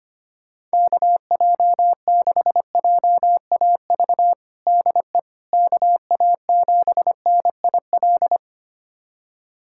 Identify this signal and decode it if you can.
Morse code, heard as a keyed tone, about 25 wpm, 700 Hz. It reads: KJ6JAV DE KA7NIL
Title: USA Callsign DE Callsign 25wpm